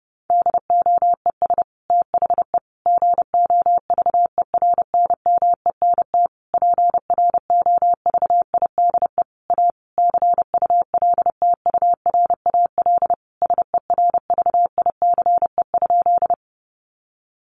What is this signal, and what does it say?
Morse code, heard as a keyed tone, about 30 wpm, 700 Hz. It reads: DOES THE GOVERNMENT PROVIDE A CULTURAL SERVICE?
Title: Sentences from Top 1500 Words 30wpm